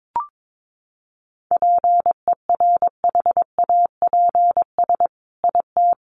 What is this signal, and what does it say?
Morse code, heard as a keyed tone, about 22 wpm, 700 Hz. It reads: PERHAPS IT
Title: Sentences from Top 600 Words 22wpm